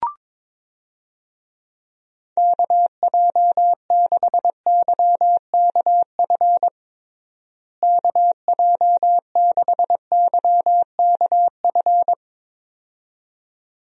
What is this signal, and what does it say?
Morse code, heard as a keyed tone, about 22 wpm, 700 Hz. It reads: KJ6YKF KJ6YKF
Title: USA Call Signs - 6 Characters - Repeated 2x - 22wpm